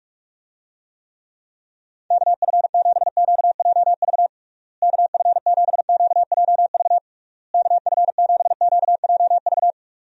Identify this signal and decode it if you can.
Morse code, heard as a keyed tone, about 45 wpm, 700 Hz. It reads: KF7QJV KF7QJV KF7QJV
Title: USA Call Signs - 6 Characters - Repeated 3x - 45wpm